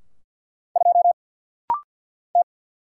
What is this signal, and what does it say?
Morse code, heard as a keyed tone, about 50 wpm, 700 Hz. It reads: 2 T
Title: Single Letter-Number Alt - Rapid-Fire 50wpm